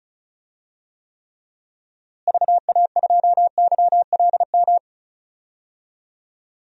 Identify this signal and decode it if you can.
Morse code, heard as a keyed tone, about 34 wpm, 700 Hz. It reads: VA2YLM
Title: DXCC-100 Call Signs - 6 Characters - Repeated 2x - 35wpm